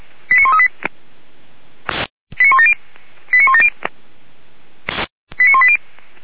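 Five-tone selective call (ZVEI), heard as quick runs of five tones, tones of about 70 ms. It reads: 89138 89170 89170 89180
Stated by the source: ZVEI five-tone selective call 89136 received on an SDR, from audio.f85360.feuerwehr-orlamuende.org (Feuerwehr Orlamünde fire service).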